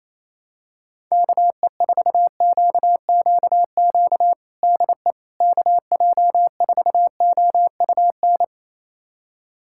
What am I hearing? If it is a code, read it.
Morse code, heard as a keyed tone, about 28 wpm, 700 Hz. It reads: KE4QQQ DE KJ4OUN
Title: USA Callsign DE Callsign - 6 Characters - 28wpm